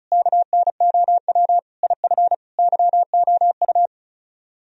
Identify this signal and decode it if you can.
Morse code, heard as a keyed tone, about 46 wpm, 700 Hz. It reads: KNOW IF YOU